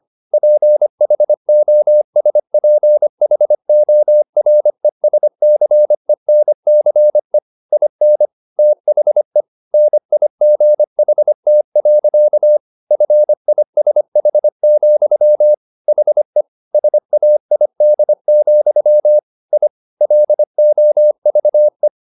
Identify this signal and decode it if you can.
Morse code, heard as a keyed tone, about 25 wpm, 600 Hz. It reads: PHOSPHORESCENCE IN THE NIGHT. FISH, HE SAID, I LOVE
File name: the old man and the sea part 1 25wpm